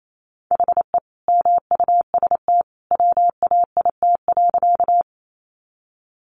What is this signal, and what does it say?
Morse code, heard as a keyed tone, about 28 wpm, 700 Hz. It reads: HE MUST WAIT.
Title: Sentences from Top 500 Words 28wpm